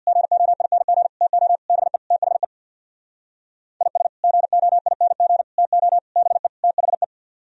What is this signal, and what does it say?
Morse code, heard as a keyed tone, about 64 wpm, 700 Hz. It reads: GOING TO BE THE IS GOING TO BE THE